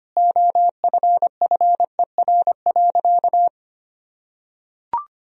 Morse code, heard as a keyed tone, about 25 wpm, 690 Hz. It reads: OFFER.
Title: Sentences from Top 1000 Words - Review Part 4 - 25wpm